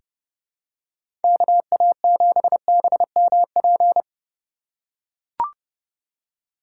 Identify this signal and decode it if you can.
Morse code, heard as a keyed tone, about 30 wpm, 700 Hz. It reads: KA7BMP
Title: USA Call Signs - 6 Characters - 2x Character Spacing - 30wpm